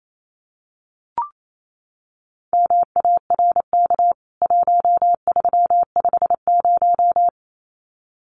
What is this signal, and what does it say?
Morse code, heard as a keyed tone, about 28 wpm, 700 Hz. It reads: MARK 1350
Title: CWT Exchange 28wpm